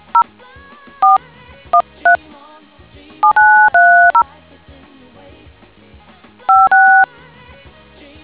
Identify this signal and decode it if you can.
DTMF, heard as telephone touch tones, tones of about 140 ms. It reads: 0 4 13 7930 56